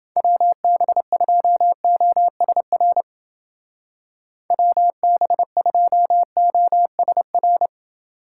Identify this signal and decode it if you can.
Morse code, heard as a keyed tone, about 29 wpm, 700 Hz. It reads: WB2OSR WB2OSR